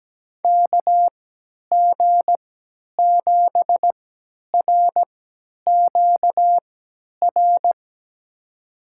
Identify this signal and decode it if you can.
Morse code, heard as a keyed tone, about 17 wpm, spaced as for 10 wpm, 700 Hz. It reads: KG7RQR
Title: USA Call Signs - 6 Characters - 3x Character Spacing - 17wpm